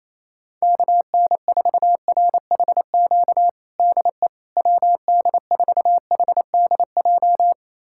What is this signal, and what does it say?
Morse code, heard as a keyed tone, about 28 wpm, 700 Hz. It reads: KN4RHQ DE WD4HDJ
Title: USA Callsign DE Callsign - 6 Characters - 28wpm